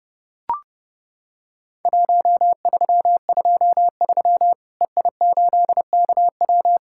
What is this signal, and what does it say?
Morse code, heard as a keyed tone, about 32 wpm, 700 Hz. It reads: 1323 EI8KW
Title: Worked All Europe DX Contest - QTC Exchange - 30wpm